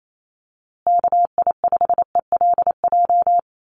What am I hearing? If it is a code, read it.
Morse code, heard as a keyed tone, about 28 wpm, 700 Hz. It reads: KI5ELJ
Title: USA Call Signs Repeated 2x 28wpm